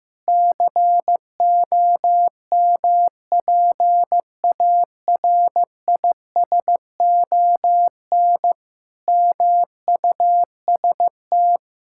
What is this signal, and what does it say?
Morse code, heard as a keyed tone, about 16 wpm, 700 Hz. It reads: COMPARISON MUST
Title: Sentences from Top 1600 Words 15wpm